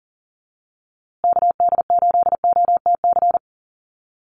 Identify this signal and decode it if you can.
Morse code, heard as a keyed tone, about 40 wpm, 700 Hz. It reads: KD8OTC
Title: USA Call Signs Repeated 3x 40wpm